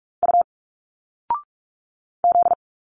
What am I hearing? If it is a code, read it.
Morse code, heard as a keyed tone, about 46 wpm, 700 Hz. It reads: U Z
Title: Single Letters 45wpm